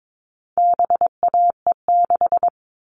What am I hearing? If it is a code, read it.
Morse code, heard as a keyed tone, about 22 wpm, 700 Hz. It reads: BAE6
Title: All Cut Numbers - 4-Digit 22wpm